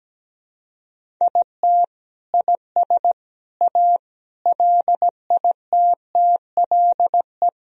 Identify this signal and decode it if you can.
Morse code, heard as a keyed tone, about 17 wpm, 700 Hz. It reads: IT IS A LITTLE
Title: Sentences from Top 1000 Words - Review Part 1 - 17wpm